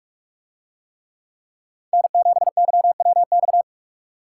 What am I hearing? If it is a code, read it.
Morse code, heard as a keyed tone, about 44 wpm, 700 Hz. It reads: N7YWX